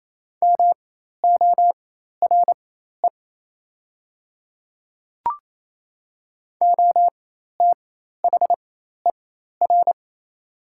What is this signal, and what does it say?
Morse code, heard as a keyed tone, about 28 wpm, spaced as for 13 wpm, 700 Hz. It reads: MORE OTHER
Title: 4x Character Spacing - Top 100 Words - 28wpm